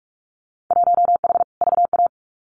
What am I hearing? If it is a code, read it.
Morse code, heard as a keyed tone, about 46 wpm, 700 Hz. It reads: 1H VA
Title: Winter Field Day Exchanges 45wpm